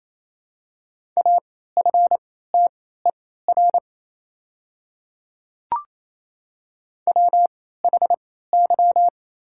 Morse code, heard as a keyed tone, about 28 wpm, spaced as for 16 wpm, 700 Hz. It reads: AFTER WHY
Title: 3x Character Spacing - Top 200 Words - 28wpm